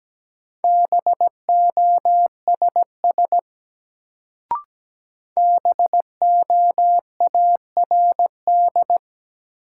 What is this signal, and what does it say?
Morse code, heard as a keyed tone, about 17 wpm, 700 Hz. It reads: BOSS BOARD
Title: Sentences from Top 700 Words - New Word Drill - 17wpm